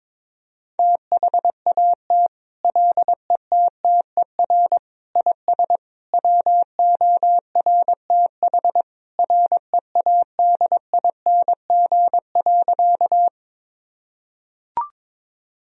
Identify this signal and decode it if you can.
Morse code, heard as a keyed tone, about 22 wpm, 700 Hz. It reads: THAT LETTER IS WORTH READING.
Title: Sentences from Top 1000 Words - Review Part 3 - 22wpm